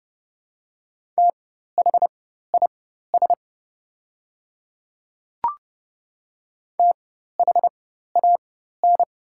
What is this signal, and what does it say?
Morse code, heard as a keyed tone, about 30 wpm, spaced as for 14 wpm, 700 Hz. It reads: THIS THAN
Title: 4x Character Spacing - Top 100 Words - 30wpm